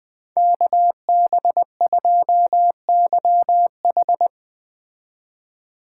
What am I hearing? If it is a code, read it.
Morse code, heard as a keyed tone, about 20 wpm, 700 Hz. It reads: KB2YH